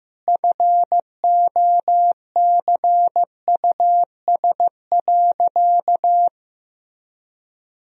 Morse code, heard as a keyed tone, about 15 wpm, 700 Hz. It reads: FOCUS.
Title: Sentences from Top 1000 Words - Review Part 4 - 15wpm